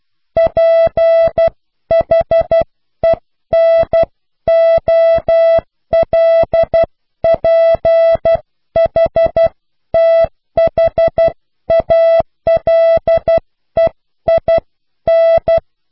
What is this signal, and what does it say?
Morse code, heard as a keyed tone, about 12 wpm, 660 Hz. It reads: PHENOLPHTHALEIN
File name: cw / Pseudo woorden (15)